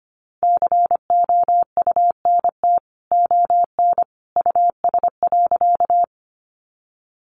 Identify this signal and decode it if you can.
Morse code, heard as a keyed tone, about 25 wpm, 700 Hz. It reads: COUNT ON US.
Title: Sentences from Top 1500 Words 25wpm